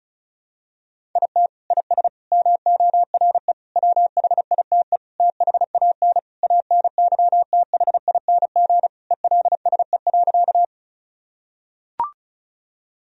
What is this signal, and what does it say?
Morse code, heard as a keyed tone, about 34 wpm, 700 Hz. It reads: IT IS MORE WHITE THAN ANYTHING ELSE.
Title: Sentences from Top 1000 Words - Review Part 1 - 35wpm